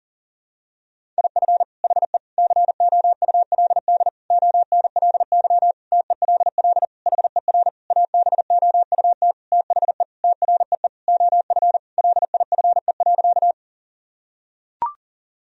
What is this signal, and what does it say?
Morse code, heard as a keyed tone, about 40 wpm, 700 Hz. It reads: IF HE COULD ONLY TELL HER ABOUT THE TREE OF LIFE.
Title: Sentences from Top 1000 Words - Review Part 1 - 40wpm